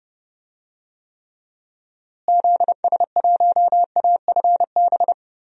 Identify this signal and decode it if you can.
Morse code, heard as a keyed tone, about 30 wpm, 700 Hz. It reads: ZS1AFB
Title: DXCC-100 Call Signs - 6 Characters - Repeated 3x - 30wpm